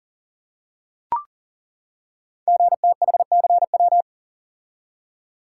Morse code, heard as a keyed tone, about 42 wpm, 700 Hz. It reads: GTHCW
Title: Code Groups - 5 Letters - 40wpm